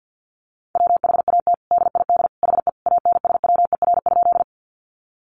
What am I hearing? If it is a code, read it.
Morse code, heard as a keyed tone, about 50 wpm, 700 Hz. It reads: WHAT DID HE ANSWER?
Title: Sentences from Top 300 Words 50wpm